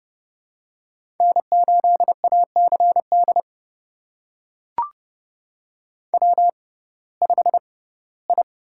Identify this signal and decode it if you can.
Morse code, heard as a keyed tone, about 30 wpm, spaced as for 18 wpm, 700 Hz. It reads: N8ACD W5I
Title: USA Call Signs - 5 Characters - 6x Character Spacing - 30wpm